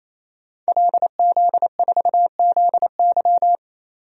Morse code, heard as a keyed tone, about 28 wpm, 690 Hz. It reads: LZ4ZY
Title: DXCC-100 Call Signs - 5 Characters - 28wpm